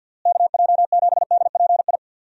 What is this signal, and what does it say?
Morse code, heard as a keyed tone, about 51 wpm, 700 Hz. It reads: KJ7DPI
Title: USA Call Signs - 6 Characters - Repeated 3x - 50wpm